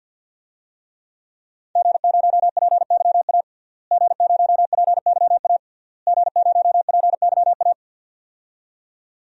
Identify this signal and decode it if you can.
Morse code, heard as a keyed tone, about 50 wpm, 700 Hz. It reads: G0PYA G0PYA G0PYA
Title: DXCC-100 Call Signs - 5 Characters - Repeated 3x - 50wpm